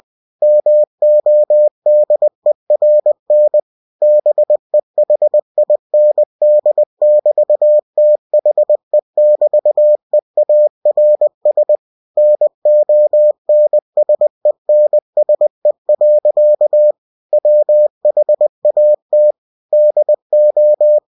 Morse code, heard as a keyed tone, about 20 wpm, 600 Hz. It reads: MODERN BEHIND<BT>THE<BT>EARS NONSENSE. WHAT DO
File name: cw-books / winnie-the-pooh / winnie-20wpm-page-0085